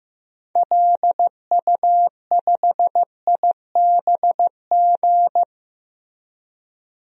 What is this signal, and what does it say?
Morse code, heard as a keyed tone, about 15 wpm, 700 Hz. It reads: LU5IBG
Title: DXCC-100 Call Signs - 6 Characters - 15wpm